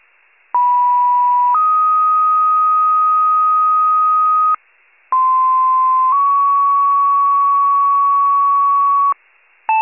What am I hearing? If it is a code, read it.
Two-tone page, heard as a one-second tone, then a three-second tone, about 980 then 1250 Hz, about 1040 then 1120 Hz. It is heard from 0.5 to 9.1 s.